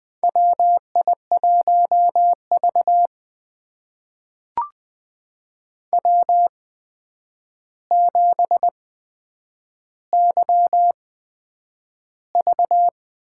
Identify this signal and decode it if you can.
Morse code, heard as a keyed tone, about 20 wpm, spaced as for 9 wpm, 700 Hz. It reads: WI1V W7YV
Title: USA Call Signs - 4 Characters - 8x Character Spacing - 20wpm